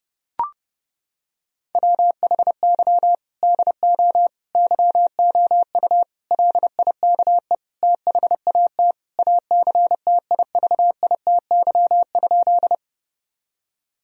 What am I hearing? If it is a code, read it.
Morse code, heard as a keyed tone, about 32 wpm, 700 Hz. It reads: WHY DO YOU LIKE THAT ACTIVITY?
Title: Sentences from Top 1000 Words - Review Part 4 - 30wpm